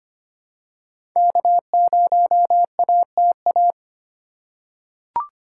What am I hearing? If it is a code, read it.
Morse code, heard as a keyed tone, about 25 wpm, 700 Hz. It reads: K0ATA
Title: USA Call Signs - 5 Characters - 4x Character Spacing - 25wpm